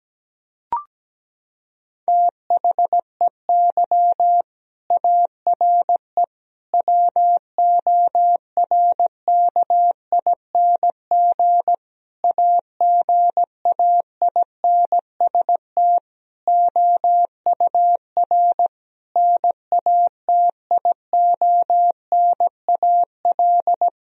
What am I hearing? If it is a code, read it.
Morse code, heard as a keyed tone, about 17 wpm, 700 Hz. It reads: THEY ARE WORKING AGAINST OUR NATIONAL
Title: Sentences from Top 1000 Words - Review Part 4 - 17wpm